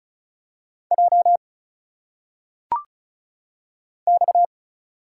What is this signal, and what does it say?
Morse code, heard as a keyed tone, about 34 wpm, 700 Hz. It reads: J X
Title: Single Letters - Phonetic Pronunciation 35wpm